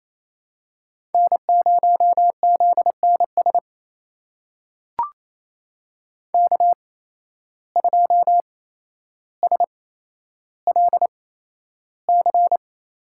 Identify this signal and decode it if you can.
Morse code, heard as a keyed tone, about 28 wpm, spaced as for 12 wpm, 700 Hz. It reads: N0ZNS K2SLC
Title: USA Call Signs - 5 Characters - 8x Character Spacing - 28wpm